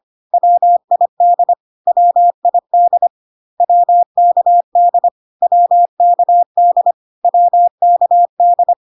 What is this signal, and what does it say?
Morse code, heard as a keyed tone, about 25 wpm, 700 Hz. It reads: WID WID WKD WKD WKD
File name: CW abbreviations alphabetical repeated 3 times 25 25wpm ik1hge